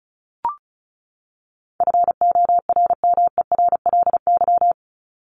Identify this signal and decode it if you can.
Morse code, heard as a keyed tone, about 34 wpm, 700 Hz. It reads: FORMERLY
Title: LY Suffix - Top 100 Words with Base Words 35wpm